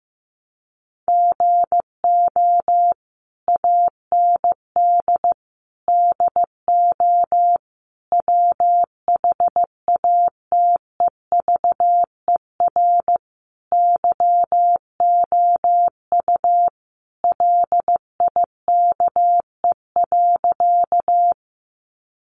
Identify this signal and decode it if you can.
Morse code, heard as a keyed tone, about 15 wpm, 700 Hz. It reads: GO AND DO WHATEVER YOU LIKE.
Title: Sentences from Top 1100 Words 15wpm